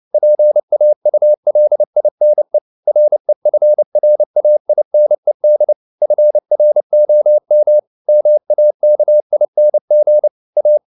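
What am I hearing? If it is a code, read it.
Morse code, heard as a keyed tone, about 29 wpm, 600 Hz. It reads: PAULINE REFRAINED FROM MAKING A
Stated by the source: Morse code at 29 wpm, sending Ken Follett's novel Never in English, from CW med Gnister.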